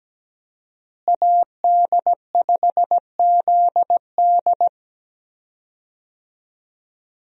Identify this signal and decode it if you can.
Morse code, heard as a keyed tone, about 17 wpm, 700 Hz. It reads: AD5ZD
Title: USA Call Signs - 5 Characters - Repeated 3x - 17wpm